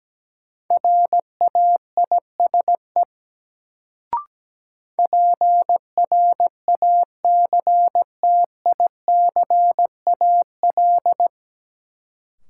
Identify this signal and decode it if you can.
Morse code, heard as a keyed tone, about 17 wpm, 700 Hz. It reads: RAISE PRACTICAL
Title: Sentences from Top 1400 Words - New Word Drill - 17wpm